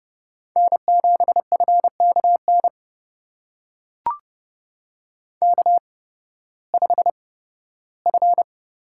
Morse code, heard as a keyed tone, about 30 wpm, spaced as for 16 wpm, 700 Hz. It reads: N7FKN K5F
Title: USA Call Signs - 5 Characters - 8x Character Spacing - 30wpm